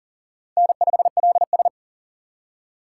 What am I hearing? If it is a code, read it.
Morse code, heard as a keyed tone, about 39 wpm, 690 Hz. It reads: N5LS
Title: USA Call Signs - 4 Characters - 6x Character Spacing - 40wpm